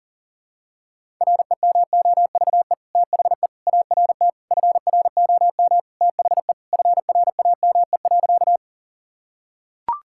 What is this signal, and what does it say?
Morse code, heard as a keyed tone, about 37 wpm, 700 Hz. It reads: REMOVE THE ART FROM THE FRAME.
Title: Sentences from Top 1000 Words - Review Part 6 - 40wpm